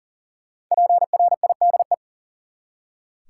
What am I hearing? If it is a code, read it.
Morse code, heard as a keyed tone, about 40 wpm, 700 Hz. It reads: PRIDE
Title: Five Letter Words - 40wpm